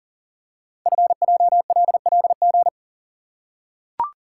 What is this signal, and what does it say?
Morse code, heard as a keyed tone, about 38 wpm, 700 Hz. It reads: FJLLG
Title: Code Groups - 5 Letters - 40wpm